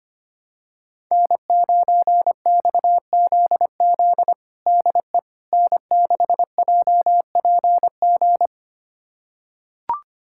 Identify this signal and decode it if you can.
Morse code, heard as a keyed tone, about 25 wpm, 700 Hz. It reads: N9XZZ DE N6JPG
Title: USA Callsign DE Callsign - 5 Characters - 25wpm